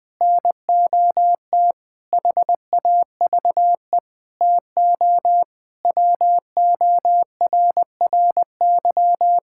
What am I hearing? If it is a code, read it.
Morse code, heard as a keyed tone, about 20 wpm, 700 Hz. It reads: NOT HAVE TO WORRY
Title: Sentences with 5-Word Phrases - 20wpm